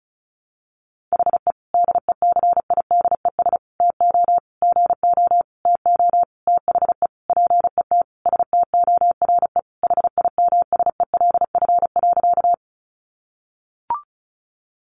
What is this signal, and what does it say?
Morse code, heard as a keyed tone, about 34 wpm, 700 Hz. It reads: HE DECIDES TO GO TO THE PET STORE HIMSELF.
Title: Easy Phrases 35wpm